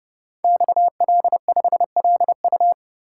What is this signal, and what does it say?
Morse code, heard as a keyed tone, about 30 wpm, 700 Hz. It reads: XL5LU
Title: Code Groups - 5 Letter:Numbers - 30wpm